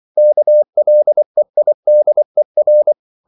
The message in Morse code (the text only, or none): KLEIDER